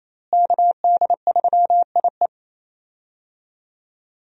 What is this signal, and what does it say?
Morse code, heard as a keyed tone, about 27 wpm, 700 Hz. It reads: KD3IE
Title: USA Call Signs - 5 Characters - Repeated 3x - 28wpm